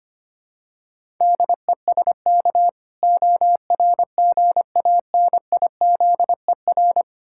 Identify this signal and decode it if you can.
Morse code, heard as a keyed tone, about 26 wpm, 700 Hz. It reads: DESK ORGANIZER
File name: Easy Words - Speed-Racing with Hints - 25wpm